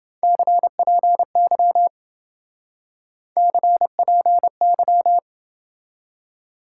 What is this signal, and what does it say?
Morse code, heard as a keyed tone, about 27 wpm, 700 Hz. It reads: CPY CPY